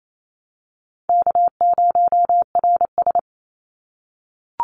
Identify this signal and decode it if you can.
Morse code, heard as a keyed tone, about 28 wpm, 700 Hz. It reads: K0RS
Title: NA SOTA Chaser Callsigns 28wpm